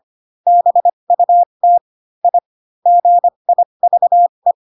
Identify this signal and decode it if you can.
Morse code, heard as a keyed tone, about 25 wpm, spaced as for 22 wpm, 700 Hz. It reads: BUT I GIVE
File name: Pinocchio / Pinocchio ch7 18 25 ik1hge